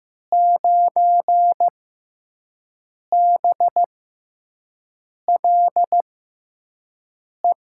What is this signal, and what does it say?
Morse code, heard as a keyed tone, about 15 wpm, spaced as for 5 wpm, 700 Hz. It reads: 9BLE
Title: USA Call Signs - 6 Characters - 6x Character Spacing - 15wpm